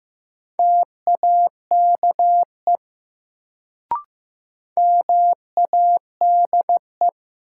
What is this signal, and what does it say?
Morse code, heard as a keyed tone, about 15 wpm, 700 Hz. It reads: TAKE MADE
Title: Sentences from Top 200 Words - New Word Drill - 15wpm